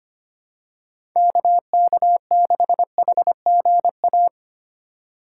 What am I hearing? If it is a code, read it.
Morse code, heard as a keyed tone, about 25 wpm, 700 Hz. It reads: KK6HGA
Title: USA Call Signs - 6 Characters - 2x Character Spacing - 25wpm